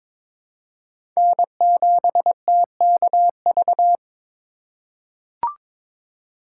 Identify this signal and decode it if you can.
Morse code, heard as a keyed tone, about 22 wpm, 700 Hz. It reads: N7TKV
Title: USA Call Signs - 5 Characters - 4x Character Spacing - 22wpm